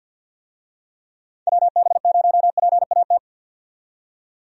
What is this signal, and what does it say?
Morse code, heard as a keyed tone, about 53 wpm, 700 Hz. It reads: WB0PAT
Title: USA Call Signs - 6 Characters - 50wpm